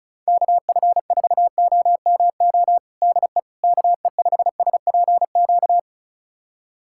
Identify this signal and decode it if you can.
Morse code, heard as a keyed tone, about 36 wpm, 700 Hz. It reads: KF4OMO DE KE5SPQ
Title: USA Callsign DE Callsign - 6 Characters - 35wpm